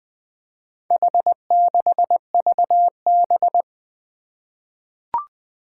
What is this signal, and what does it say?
Morse code, heard as a keyed tone, about 20 wpm, 700 Hz. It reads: H6VB
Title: Four Challenging Letter-Number Combinations 20wpm